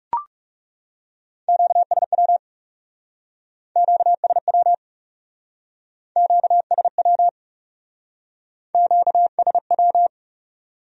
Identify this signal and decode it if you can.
Morse code, heard as a keyed tone, about 39 wpm, 700 Hz. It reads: QSW QSW QSW QSW